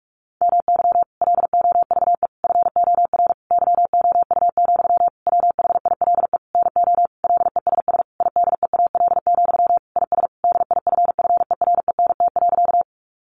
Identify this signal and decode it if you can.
Morse code, heard as a keyed tone, about 44 wpm, 700 Hz. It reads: MY LOVE FOR YOU, WHILE NO LESS IDEAL, IS DIFFERENT.
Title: Sentences from Top 1400 Words 45wpm